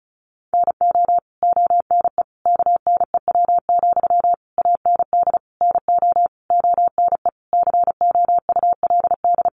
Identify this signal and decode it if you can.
Morse code, heard as a keyed tone, about 34 wpm, 700 Hz. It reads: NO ONE KNEW, AND NO ONE COULD
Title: Sentences from Top 1400 Words 35wpm